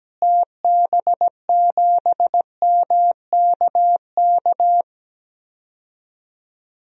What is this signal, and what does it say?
Morse code, heard as a keyed tone, about 17 wpm, 700 Hz. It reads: TB7MKK
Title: DXCC-100 Call Signs - 6 Characters - Repeated 2x - 17wpm